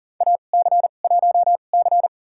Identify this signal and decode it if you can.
Morse code, heard as a keyed tone, about 40 wpm, spaced as for 29 wpm, 700 Hz. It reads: AC1C
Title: USA Call Signs - 2x Character Spacing - 40wpm